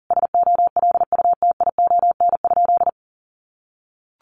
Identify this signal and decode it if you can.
Morse code, heard as a keyed tone, about 37 wpm, 700 Hz. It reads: SOLUTION?
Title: Sentences from Top 1300 Words 40wpm